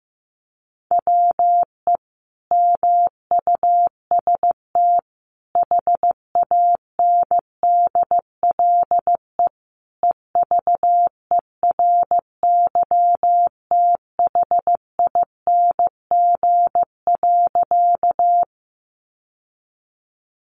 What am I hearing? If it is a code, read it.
Morse code, heard as a keyed tone, about 14 wpm, 700 Hz. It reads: WE MUST HANDLE EVERYTHING.
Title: Sentences from Top 1100 Words 15wpm